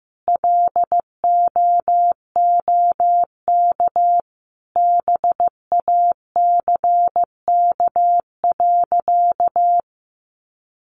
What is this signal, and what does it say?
Morse code, heard as a keyed tone, about 14 wpm, 700 Hz. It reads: LOOK BACK.
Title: Sentences from Top 300 Words 15wpm